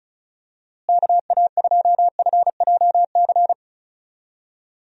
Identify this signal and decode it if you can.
Morse code, heard as a keyed tone, about 34 wpm, 700 Hz. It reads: KA2FJC
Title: USA Call Signs - 6 Characters - Repeated 2x - 35wpm